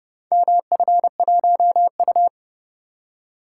Morse code, heard as a keyed tone, about 26 wpm, 700 Hz. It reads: MF1U